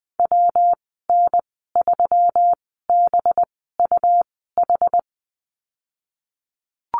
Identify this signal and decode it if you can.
Morse code, heard as a keyed tone, about 20 wpm, spaced as for 15 wpm, 700 Hz. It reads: WN3BUH
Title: 193 Lesson 31g - Letter v - Call Signs 20wpm